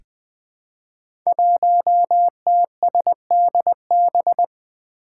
Morse code, heard as a keyed tone, about 20 wpm, 700 Hz. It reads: 1TSDB
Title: Five Challenging Letter-Number Combinations 20wpm